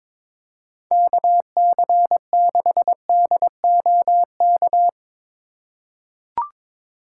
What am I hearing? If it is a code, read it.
Morse code, heard as a keyed tone, about 21 wpm, 700 Hz. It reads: KC6DOK
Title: USA Call Signs - 6 Characters - 3x Character Spacing - 22wpm